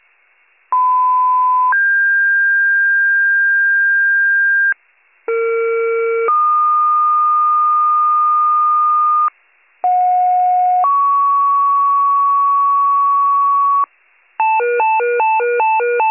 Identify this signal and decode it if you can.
Two-tone page, heard as a one-second tone, then a three-second tone, about 1010 then 1600 Hz, about 460 then 1220 Hz, about 730 then 1120 Hz. It is heard from 0.7 to 13.9 s.